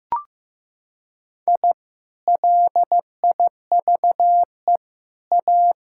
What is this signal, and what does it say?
Morse code, heard as a keyed tone, about 14 wpm, 700 Hz. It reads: I LIVE A
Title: Sentences from Top 1000 Words - Review Part 6 - 15wpm